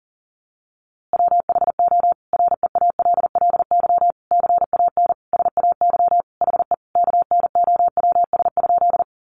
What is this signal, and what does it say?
Morse code, heard as a keyed tone, about 37 wpm, 700 Hz. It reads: WHO REALLY CAN SAY HE KNOWS?
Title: A Sound Of Thunder 40wpm